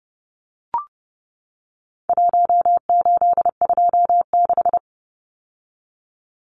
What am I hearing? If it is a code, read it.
Morse code, heard as a keyed tone, about 29 wpm, 700 Hz. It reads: 1826
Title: Numbers - Contest Style - 4-Digit 30wpm